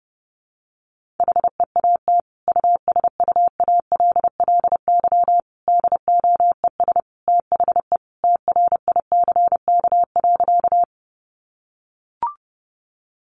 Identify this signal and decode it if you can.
Morse code, heard as a keyed tone, about 30 wpm, 700 Hz. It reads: HEAT USUALLY DOES THE TRICK.
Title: Sentences from Top 1500 Words 30wpm